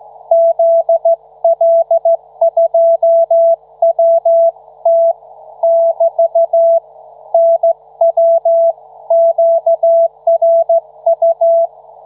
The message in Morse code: ZL2WT <BT> NW QRU